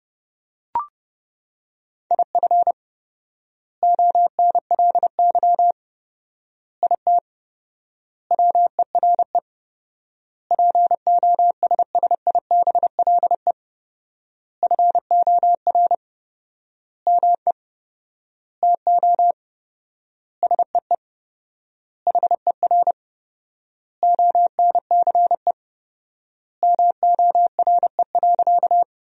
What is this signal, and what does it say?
Morse code, heard as a keyed tone, about 30 wpm, 700 Hz. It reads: IF ONLY IT WERE POSSIBLE FOR ME TO SEE HER ONCE MORE.
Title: 4x Word-Spacing - Sentences from Top 500 Words - 30wpm